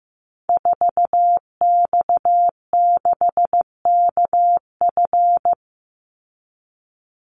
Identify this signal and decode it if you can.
Morse code, heard as a keyed tone, about 15 wpm, 700 Hz. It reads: 4X6KF
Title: Call Signs Repeated 2x - 15wpm